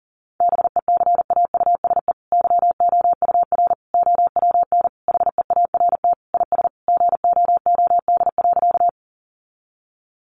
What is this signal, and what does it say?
Morse code, heard as a keyed tone, about 40 wpm, 700 Hz. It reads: BECAUSE YOUR OWN HEART IS GOOD.
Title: Sentences from Top 1200 Words 40wpm